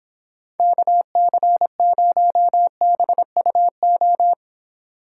Morse code, heard as a keyed tone, about 29 wpm, 700 Hz. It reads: KC0BUO